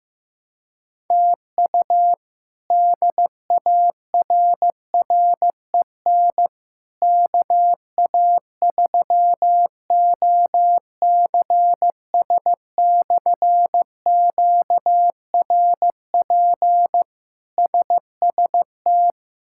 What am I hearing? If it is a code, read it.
Morse code, heard as a keyed tone, about 15 wpm, 700 Hz. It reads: TU DARREN KA3OCS/QRP SST
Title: K1USN SST QSO Practice - 15wpm